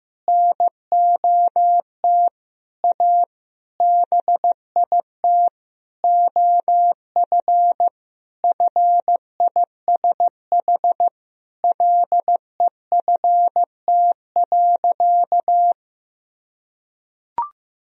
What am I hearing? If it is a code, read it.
Morse code, heard as a keyed tone, about 15 wpm, 700 Hz. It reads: NOT A BIT OF FISH LEFT.
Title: Sentences from Top 1000 Words - Review Part 1 - 15wpm